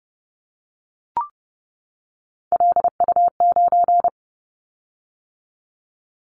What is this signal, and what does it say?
Morse code, heard as a keyed tone, about 30 wpm, 700 Hz. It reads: LU9
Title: Callsign Prefixes for Top 100 DXCC - 30wpm